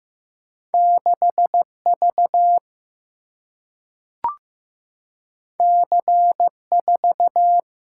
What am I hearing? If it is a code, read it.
Morse code, heard as a keyed tone, about 15 wpm, 700 Hz. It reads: 6 V C 4
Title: Single-Letter Single-Number 15wpm